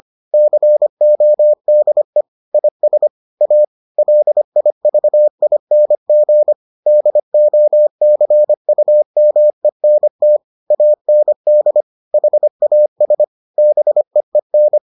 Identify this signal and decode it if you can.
Morse code, heard as a keyed tone, about 25 wpm, 600 Hz. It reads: CODE IS A LIVING DOCUMENT AND HAS BEEN